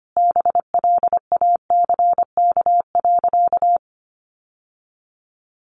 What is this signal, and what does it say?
Morse code, heard as a keyed tone, about 25 wpm, 700 Hz. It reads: BLACK.
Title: Sentences from Top 1200 Words 25wpm